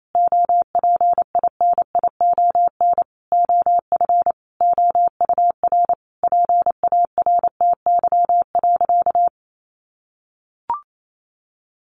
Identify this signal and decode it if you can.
Morse code, heard as a keyed tone, about 28 wpm, 700 Hz. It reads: OPINION OF OUR PARTY.
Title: Sentences from Top 1400 Words 28wpm